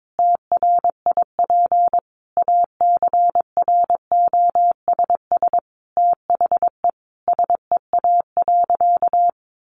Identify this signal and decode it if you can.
Morse code, heard as a keyed tone, about 22 wpm, 700 Hz. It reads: TRIP ACROSS THE SEA.
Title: Sentences from Top 1300 Words 22wpm